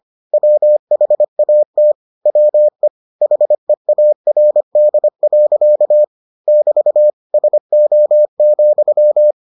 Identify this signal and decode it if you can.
Morse code, heard as a keyed tone, about 25 wpm, 600 Hz. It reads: WHAT WE HEARD. <BT> SO,